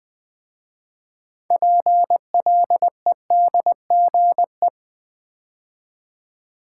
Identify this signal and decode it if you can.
Morse code, heard as a keyed tone, about 20 wpm, 700 Hz. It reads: PLEDGE
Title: 6 Letter Words - First 3 Letters Given - 20wpm